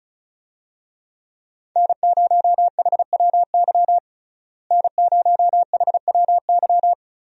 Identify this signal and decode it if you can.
Morse code, heard as a keyed tone, about 34 wpm, 700 Hz. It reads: N0HWY N0HWY
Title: USA Call Signs - 5 Characters - Repeated 3x - 35wpm